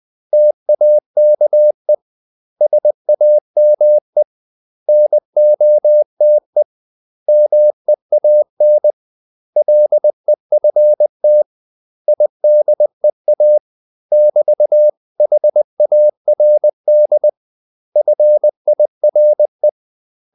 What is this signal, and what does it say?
Morse code, heard as a keyed tone, about 20 wpm, 600 Hz. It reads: TAKE SAME NOTE MEAN LEFT IDEA <BT>HARD FIRE
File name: CW / Practice / 500MostCommonEnglishWords20